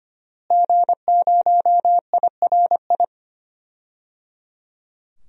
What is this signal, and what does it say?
Morse code, heard as a keyed tone, about 25 wpm, 700 Hz. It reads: G0IRI
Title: DXCC-100 Call Signs - 5 Characters - Repeated 3x - 25wpm